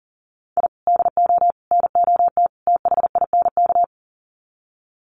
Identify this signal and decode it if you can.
Morse code, heard as a keyed tone, about 40 wpm, 700 Hz. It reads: I DO NOT THINK
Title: Sets of 4 Words 40wpm